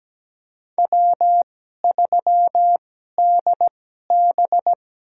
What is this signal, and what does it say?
Morse code, heard as a keyed tone, about 17 wpm, spaced as for 12 wpm, 700 Hz. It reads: W3DB
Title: USA Call Signs - 4 Characters - 2x Character Spacing - 17wpm